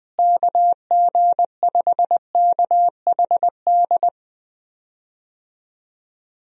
Morse code, heard as a keyed tone, about 20 wpm, 700 Hz. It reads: KG5KHD
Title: USA Call Signs - 6 Characters - Repeated 3x - 20wpm